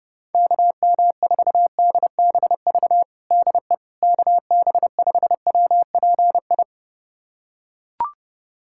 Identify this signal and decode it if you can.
Morse code, heard as a keyed tone, about 30 wpm, 700 Hz. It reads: KM4DBV DE KB5WPI
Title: USA Callsign DE Callsign - 6 Characters - 30wpm